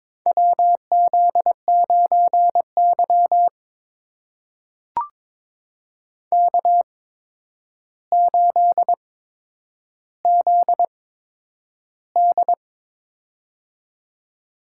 Morse code, heard as a keyed tone, about 22 wpm, spaced as for 9 wpm, 700 Hz. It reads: WZ9Y K8ZD
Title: USA Call Signs - 4 Characters - 8x Character Spacing - 22wpm